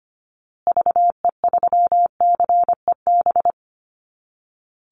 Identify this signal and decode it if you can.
Morse code, heard as a keyed tone, about 25 wpm, 700 Hz. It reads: VE3CEB
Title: Call Signs Repeated 2x - 25wpm